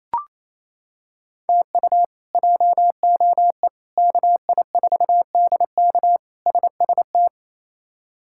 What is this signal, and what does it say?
Morse code, heard as a keyed tone, about 28 wpm, 700 Hz. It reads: TU JOE KI4DK SST
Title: K1USN SST QSO Practice - 28wpm